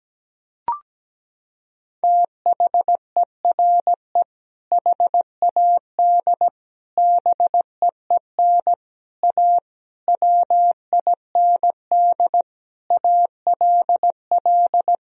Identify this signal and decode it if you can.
Morse code, heard as a keyed tone, about 17 wpm, 700 Hz. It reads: THERE HAD BEEN A WIND ALL
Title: Sentences from Top 1000 Words - Review Part 2 - 17wpm